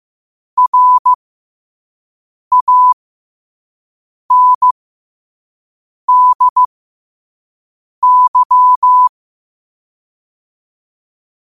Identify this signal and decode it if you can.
Morse code, heard as a keyed tone, about 15 wpm, spaced as for 5 wpm, 1000 Hz. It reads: RANDY
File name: qotdfiles / 5wpm